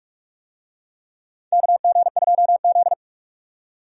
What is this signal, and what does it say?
Morse code, heard as a keyed tone, about 42 wpm, 700 Hz. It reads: KG2Z
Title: USA Call Signs - 4 Characters - 45wpm